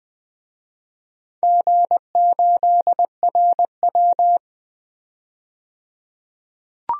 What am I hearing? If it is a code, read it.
Morse code, heard as a keyed tone, about 20 wpm, 700 Hz. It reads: G8RW
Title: DXCC-100 Call Signs - 4 Characters - 20wpm